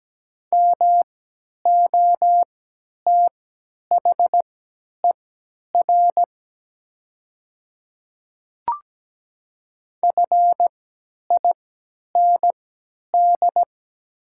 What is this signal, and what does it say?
Morse code, heard as a keyed tone, about 17 wpm, spaced as for 10 wpm, 700 Hz. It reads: MOTHER FIND
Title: 3x Character Spacing - Top 200 Words - 17wpm